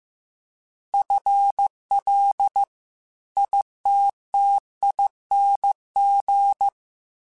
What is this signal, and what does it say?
Morse code, heard as a keyed tone, about 15 wpm, 800 Hz. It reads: FL ITTING
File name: FL / FL mp3s / FL Prefixes 3x2 sp @13